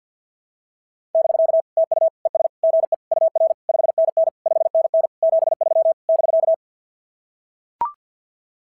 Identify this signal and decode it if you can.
Morse code, heard as a keyed tone, about 46 wpm, 660 Hz. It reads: <BK> TU ES GE UR 5NN 5NN 73 <BK>
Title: SOTA QSO Practice 50wpm